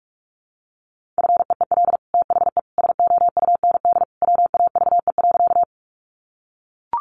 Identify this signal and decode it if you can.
Morse code, heard as a keyed tone, about 44 wpm, 700 Hz. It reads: FEEL THE SOUND WAVE.
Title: Sentences from Top 1400 Words 45wpm